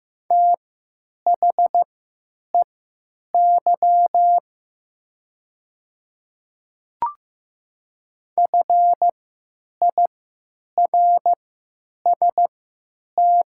Morse code, heard as a keyed tone, about 15 wpm, spaced as for 9 wpm, 700 Hz. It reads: THEY FIRST
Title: 3x Character Spacing - Top 500 Words - 15wpm